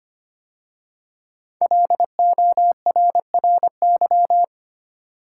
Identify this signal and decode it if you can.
Morse code, heard as a keyed tone, about 25 wpm, 700 Hz. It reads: LORRY
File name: 5 Letter Words - First 3 Letters Given - 25wpm